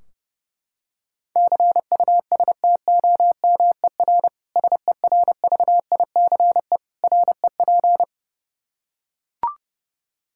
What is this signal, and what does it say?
Morse code, heard as a keyed tone, about 30 wpm, 700 Hz. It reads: CUSTOMER SERVICE REP
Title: K1USN SST QSO Practice 30wpm